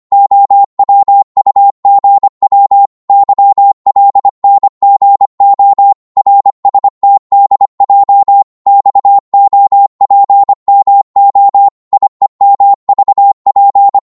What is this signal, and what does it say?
Morse code, heard as a keyed tone, about 25 wpm, 800 Hz. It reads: OWUGW YLNGO RSTDJ XOPMO IEMVP
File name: Morse / ltr-25wpm